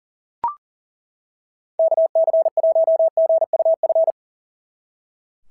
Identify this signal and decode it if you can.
Morse code, heard as a keyed tone, about 40 wpm, 690 Hz. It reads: KC1GUF
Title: SOTA QSO Practice 40wpm